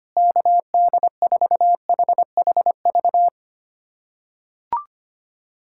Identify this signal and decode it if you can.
Morse code, heard as a keyed tone, about 25 wpm, 700 Hz. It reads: KD4HHV
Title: USA Callsign DE Callsign 25wpm